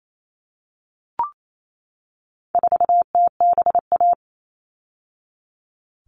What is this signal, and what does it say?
Morse code, heard as a keyed tone, about 28 wpm, 700 Hz. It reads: 4TBA
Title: All Cut Numbers - 4-Digit 28wpm